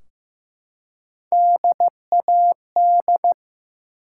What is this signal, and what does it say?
Morse code, heard as a keyed tone, about 15 wpm, 700 Hz. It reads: DAD
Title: Three Letter Words - 15wpm